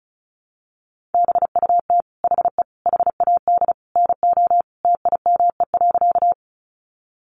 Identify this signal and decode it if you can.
Morse code, heard as a keyed tone, about 34 wpm, 700 Hz. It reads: BUT HE HAD NO TIME.
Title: Sentences from Top 100 Words 35wpm